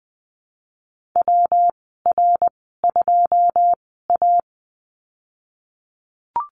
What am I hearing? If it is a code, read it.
Morse code, heard as a keyed tone, about 20 wpm, spaced as for 14 wpm, 700 Hz. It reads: WR2A
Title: 122 Lesson 20g - Number 3 - Call Signs 20wpm